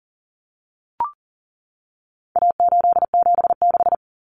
Morse code, heard as a keyed tone, about 40 wpm, 700 Hz. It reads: A876
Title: Common Cut Numbers - 4-Digit 40wpm